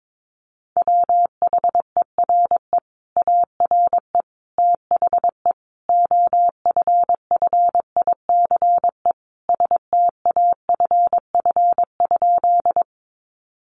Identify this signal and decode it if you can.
Morse code, heard as a keyed tone, about 22 wpm, 700 Hz. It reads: WHERE ARE THE OFFICE STAFF?
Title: Sentences from Top 1200 Words 22wpm